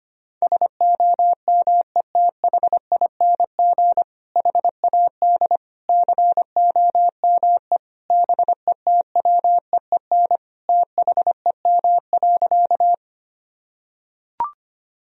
Morse code, heard as a keyed tone, about 25 wpm, 700 Hz. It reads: SOMETHING HAD COME BETWEEN THEM.
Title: Sentences from Top 1000 Words - Review Part 1 - 25wpm